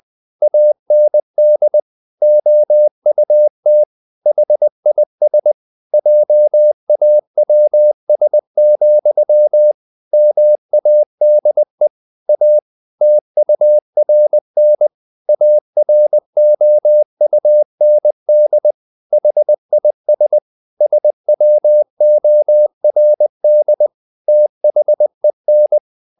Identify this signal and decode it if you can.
Morse code, heard as a keyed tone, about 20 wpm, 600 Hz. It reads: AND OUT HIS JAWS, MADE A TURN AROUND HIS SWORD THEN
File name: the old man and the sea part 2 20wpm